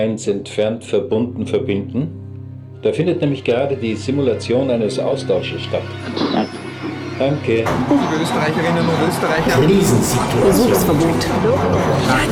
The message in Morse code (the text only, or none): none